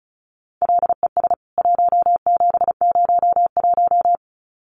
Morse code, heard as a keyed tone, about 34 wpm, 700 Hz. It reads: LES 1701
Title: CWT Exchange 35wpm